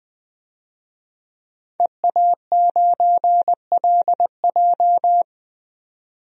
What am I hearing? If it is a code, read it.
Morse code, heard as a keyed tone, about 21 wpm, 700 Hz. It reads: EA9LJ
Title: DXCC-100 Call Signs - 5 Characters - 20wpm